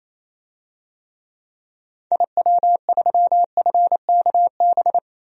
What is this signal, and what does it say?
Morse code, heard as a keyed tone, about 28 wpm, 700 Hz. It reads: IW3FKB
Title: DXCC-100 Call Signs - 6 Characters - 28wpm